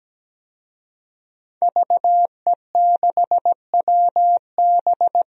Morse code, heard as a keyed tone, about 17 wpm, 700 Hz. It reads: VE6WB
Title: DXCC-100 Call Signs - 5 Characters - 17wpm